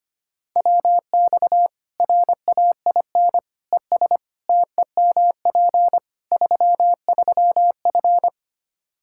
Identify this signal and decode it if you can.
Morse code, heard as a keyed tone, about 25 wpm, 700 Hz. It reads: WX RAIN ES TEMP 33F
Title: Practice QSO - Exchange 2 - Info - 25wpm